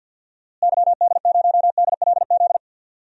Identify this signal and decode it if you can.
Morse code, heard as a keyed tone, about 49 wpm, 700 Hz. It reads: YD0DLZ
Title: DXCC-100 Call Signs - 6 Characters - Repeated 3x - 50wpm